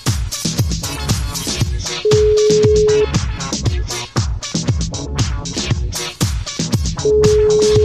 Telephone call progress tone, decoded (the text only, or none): ringback tone